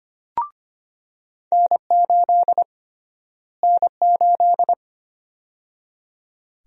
Morse code, heard as a keyed tone, about 25 wpm, 700 Hz. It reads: N8 N8
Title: USA Call Sign Prefixes - Repeated 2x - 25wpm